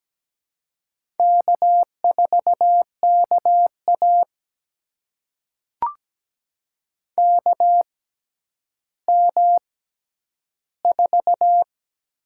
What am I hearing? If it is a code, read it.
Morse code, heard as a keyed tone, about 17 wpm, spaced as for 10 wpm, 700 Hz. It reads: K4KA KM4
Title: USA Call Signs - 6x Character Spacing - 17wpm